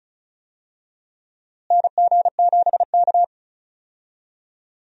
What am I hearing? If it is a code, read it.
Morse code, heard as a keyed tone, about 36 wpm, 700 Hz. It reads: NG7K
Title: USA Call Signs - 4 Characters - Repeated 2x - 35wpm